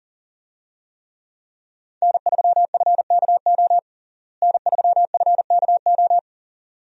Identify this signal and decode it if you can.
Morse code, heard as a keyed tone, about 40 wpm, 700 Hz. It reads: N3FKO N3FKO
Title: USA Call Signs - 5 Characters - Repeated 2x - 40wpm